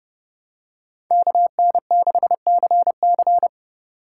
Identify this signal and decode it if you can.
Morse code, heard as a keyed tone, about 30 wpm, 700 Hz. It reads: KN6CC
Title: USA Call Signs - 5 Characters - 3x Character Spacing - 30wpm